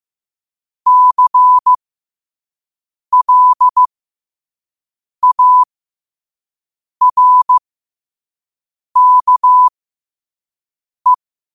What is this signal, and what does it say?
Morse code, heard as a keyed tone, about 15 wpm, spaced as for 5 wpm, 1000 Hz. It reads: CLARKE
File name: qotdfiles / 5wpm